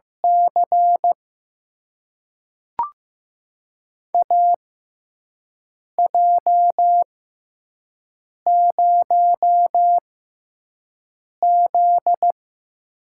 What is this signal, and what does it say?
Morse code, heard as a keyed tone, about 15 wpm, spaced as for 5 wpm, 700 Hz. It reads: C AJ0Z
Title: USA Call Signs - 4 Characters - 6x Character Spacing - 15wpm